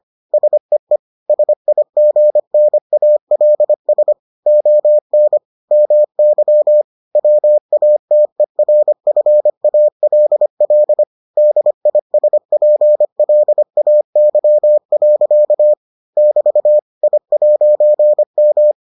Morse code, heard as a keyed tone, about 25 wpm, 600 Hz. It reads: SEE SIGNALS ON MY WATERFALL DISPLAY. <BT> I'M